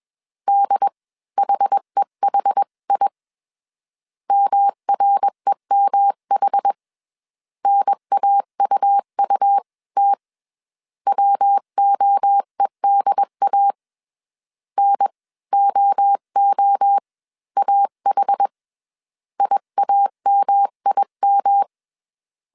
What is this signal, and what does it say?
Morse code, heard as a keyed tone, about 21 wpm, spaced as for 18 wpm, 800 Hz. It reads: DHEHI MREMH NAUUT WOEDA NOOAH IAMIM